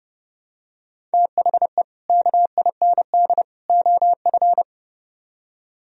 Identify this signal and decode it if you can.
Morse code, heard as a keyed tone, about 30 wpm, 700 Hz. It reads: THE KIND OF